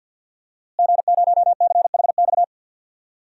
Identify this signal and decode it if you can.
Morse code, heard as a keyed tone, about 50 wpm, 700 Hz. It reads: G0CHX
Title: DXCC-100 Call Signs - 5 Characters - Repeated 2x - 50wpm